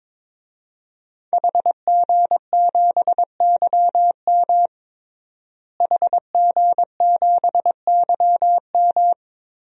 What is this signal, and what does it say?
Morse code, heard as a keyed tone, about 22 wpm, 700 Hz. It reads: HG7YM HG7YM
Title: DXCC-100 Call Signs - 5 Characters - Repeated 2x - 22wpm